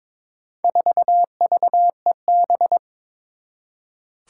Morse code, heard as a keyed tone, about 22 wpm, 700 Hz. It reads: 4VEB
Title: Four Challenging Letter-Number Combinations 22wpm